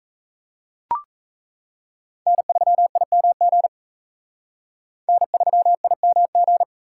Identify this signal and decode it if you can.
Morse code, heard as a keyed tone, about 36 wpm, 700 Hz. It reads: N3IMG N3IMG